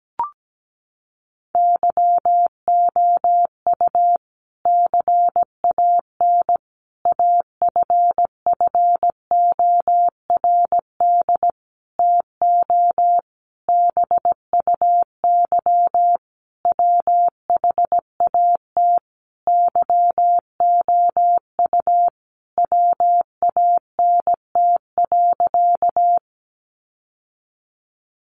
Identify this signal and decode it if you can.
Morse code, heard as a keyed tone, about 17 wpm, 700 Hz. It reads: YOU CAN AFFORD TO BUY WHAT YOU WANT.
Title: Sentences from Top 1100 Words 17wpm